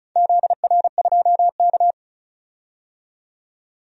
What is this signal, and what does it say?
Morse code, heard as a keyed tone, about 34 wpm, 700 Hz. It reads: ZR2K
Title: DXCC-100 Call Signs - 4 Characters - 35wpm